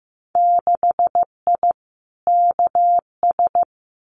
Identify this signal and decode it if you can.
Morse code, heard as a keyed tone, about 15 wpm, 700 Hz. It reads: 6I KS
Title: Winter Field Day Exchanges Encore 15wpm